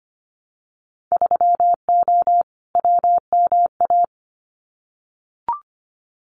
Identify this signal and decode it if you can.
Morse code, heard as a keyed tone, about 25 wpm, 700 Hz. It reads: 3O WMA
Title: Winter Field Day Exchanges Encore 25wpm